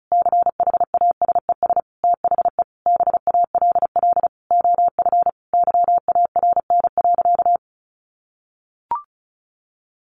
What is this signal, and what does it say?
Morse code, heard as a keyed tone, about 34 wpm, 700 Hz. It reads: CHASES THE BALL OF YARN.
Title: Additional Easy Phrases 35wpm